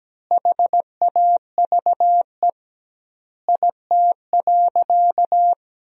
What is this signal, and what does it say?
Morse code, heard as a keyed tone, about 17 wpm, 700 Hz. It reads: HAVE IT.
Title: 2x Word-Spacing - Sentences from Top 100 Words - 17wpm